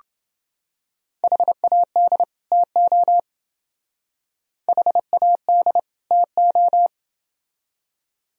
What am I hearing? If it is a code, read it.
Morse code, heard as a keyed tone, about 28 wpm, 700 Hz. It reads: HAD TO HAD TO